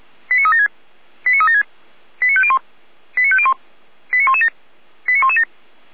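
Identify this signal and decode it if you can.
Five-tone selective call (ZVEI), heard as quick runs of five tones, tones of about 70 ms. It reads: 89376 89376 89501 89501 89117 89117
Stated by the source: ZVEI five-tone call 89080 received on SDR.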